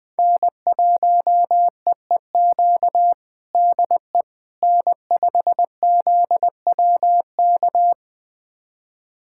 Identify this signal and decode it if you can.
Morse code, heard as a keyed tone, about 21 wpm, 700 Hz. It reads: N1EEQ DE N5ZWK
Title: USA Callsign DE Callsign - 5 Characters - 20wpm